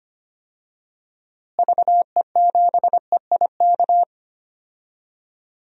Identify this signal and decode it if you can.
Morse code, heard as a keyed tone, about 25 wpm, 700 Hz. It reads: VE7EIK
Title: DXCC-100 Call Signs - 6 Characters - Repeated 3x - 25wpm